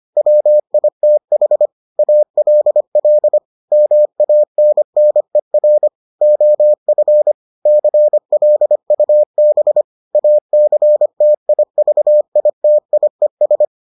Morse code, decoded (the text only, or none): WITH ALL MANNER OF CLUB ACTIVITIES